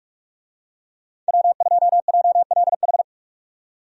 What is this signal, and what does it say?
Morse code, heard as a keyed tone, about 46 wpm, 700 Hz. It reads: W2JLH
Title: USA Call Signs - 5 Characters - 45wpm